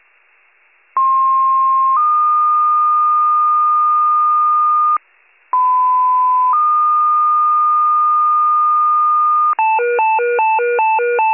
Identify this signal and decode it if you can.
Two-tone page, heard as a one-second tone, then a three-second tone, about 1060 then 1220 Hz, about 1010 then 1220 Hz. It is heard from 1.0 to 9.5 s.
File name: pager / 94 RESCUE / wcalert